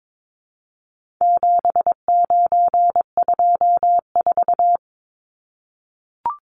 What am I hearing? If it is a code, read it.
Morse code, heard as a keyed tone, about 22 wpm, 700 Hz. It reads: 7924
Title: Numbers - 4-Digit 22wpm